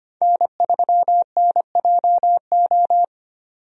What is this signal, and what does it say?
Morse code, heard as a keyed tone, about 25 wpm, 700 Hz. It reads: N3NJO
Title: USA Call Signs - 5 Characters - 25wpm